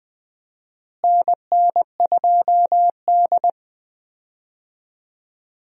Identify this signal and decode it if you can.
Morse code, heard as a keyed tone, about 20 wpm, 700 Hz. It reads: NN2D
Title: USA Call Signs - 4 Characters - Repeated 3x - 20wpm